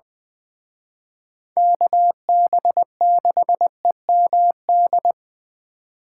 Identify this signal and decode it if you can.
Morse code, heard as a keyed tone, about 24 wpm, 700 Hz. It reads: KB6EMD
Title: Speed Racing - USA 2x3 Call Signs 20wpm